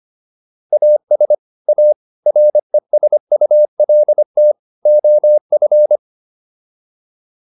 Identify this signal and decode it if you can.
Morse code, heard as a keyed tone, about 25 wpm, 600 Hz. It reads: AS A RESULT OF